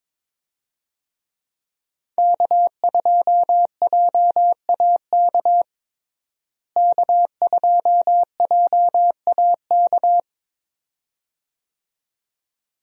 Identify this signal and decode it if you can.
Morse code, heard as a keyed tone, about 22 wpm, 700 Hz. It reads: K2JAK K2JAK
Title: USA Call Signs - 5 Characters - Repeated 2x - 22wpm